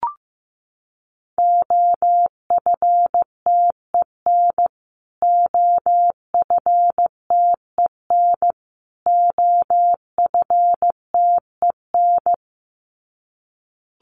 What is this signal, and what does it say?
Morse code, heard as a keyed tone, about 15 wpm, 700 Hz. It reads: OFTEN OFTEN OFTEN
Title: Top 300 Words Repeated 15wpm